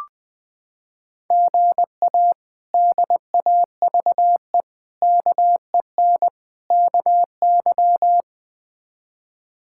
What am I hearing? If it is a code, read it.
Morse code, heard as a keyed tone, about 21 wpm, 700 Hz. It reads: GA DAVE KEN KY
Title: K1USN SST QSO Practice - 20wpm